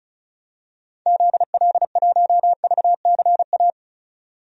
Morse code, heard as a keyed tone, about 36 wpm, 700 Hz. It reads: ZL1VCA
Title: DXCC-100 Call Signs - 6 Characters - 35wpm